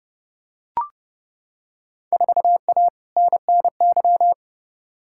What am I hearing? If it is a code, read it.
Morse code, heard as a keyed tone, about 30 wpm, 700 Hz. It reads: 4A NNY
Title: Field Day Exchanges - 30wpm